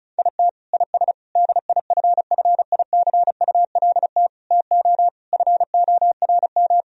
Morse code, heard as a keyed tone, about 34 wpm, 700 Hz. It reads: IT IS DIFFICULT TO FORM
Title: Sentences from Top 1000 Words - Review Part 1 - 35wpm